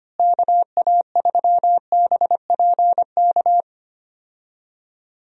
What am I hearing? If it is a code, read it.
Morse code, heard as a keyed tone, about 25 wpm, 700 Hz. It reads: KA3BPK
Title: USA Call Signs - 6 Characters - Repeated 3x - 25wpm